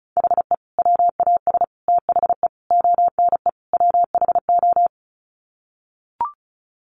Morse code, heard as a keyed tone, about 34 wpm, 700 Hz. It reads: HE WAS THE ONE WHO
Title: Sets of 5 Words Encore2 35wpm